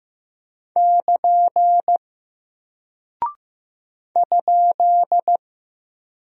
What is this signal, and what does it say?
Morse code, heard as a keyed tone, about 15 wpm, 700 Hz. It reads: <KN> ?
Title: Single Character (Letter:Number:Punctuation:Prosign) Alt 15wpm